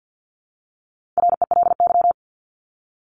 Morse code, heard as a keyed tone, about 50 wpm, 700 Hz. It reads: RELY
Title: Top 1001-1250 Words 50wpm